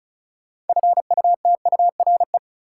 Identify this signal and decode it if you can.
Morse code, heard as a keyed tone, about 34 wpm, 700 Hz. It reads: FUTURE
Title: Sets of 3 Words - New Word Drill - 35wpm